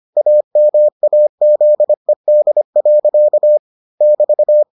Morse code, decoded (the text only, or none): AMAZED. <BT>